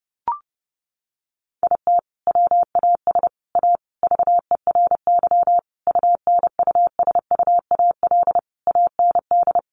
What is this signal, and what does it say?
Morse code, heard as a keyed tone, about 30 wpm, 700 Hz. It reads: IT WAS A VERY UNUSUAL AND
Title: Sentences from Top 1400 Words 30wpm